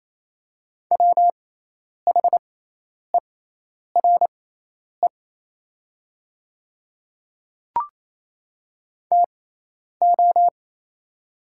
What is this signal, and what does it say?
Morse code, heard as a keyed tone, about 28 wpm, spaced as for 10 wpm, 700 Hz. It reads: WHERE TO
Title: 6x Character Spacing - Top 200 Words - 28wpm